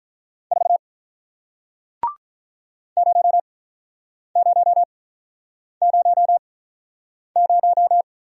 Morse code, heard as a keyed tone, about 46 wpm, spaced as for 9 wpm, 700 Hz. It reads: 4 0000